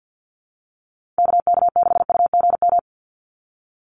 Morse code, heard as a keyed tone, about 49 wpm, 700 Hz. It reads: KK6UGM
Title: USA Call Signs 50wpm